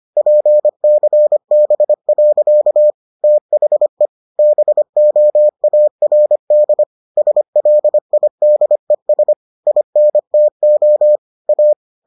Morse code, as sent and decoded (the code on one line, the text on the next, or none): .--. -.-. -... .-.-.- / - .... / -... --- .- .-. -.. / ... .-.. .. -.. ... / .. -. - --- / .-
PCB. THE BOARD SLIDES INTO A